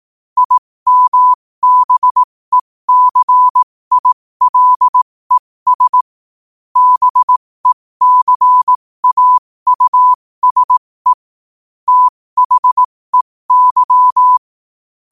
Morse code, heard as a keyed tone, about 18 wpm, spaced as for 15 wpm, 1000 Hz. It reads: IMBECILES BECAUSE THEY